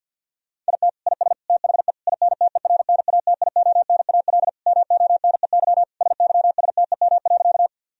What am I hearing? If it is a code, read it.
Morse code, heard as a keyed tone, about 50 wpm, 700 Hz. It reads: IT IS THE INTERNATIONAL MONEY SYSTEM.
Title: Sentences from Top 1000 Words - Review Part 4 - 50wpm